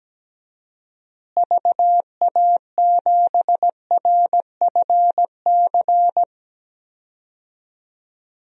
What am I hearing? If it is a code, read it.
Morse code, heard as a keyed tone, about 17 wpm, 700 Hz. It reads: VA7RFC
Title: DXCC-100 Call Signs - 6 Characters - Repeated 3x - 17wpm